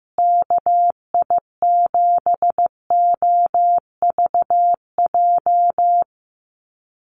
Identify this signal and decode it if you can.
Morse code, heard as a keyed tone, about 15 wpm, 700 Hz. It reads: KI7OVJ
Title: USA Call Signs Repeated 3x 15wpm